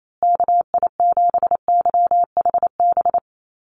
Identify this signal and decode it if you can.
Morse code, heard as a keyed tone, about 28 wpm, 690 Hz. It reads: KI7YHB
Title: USA Call Signs Repeated 3x 28wpm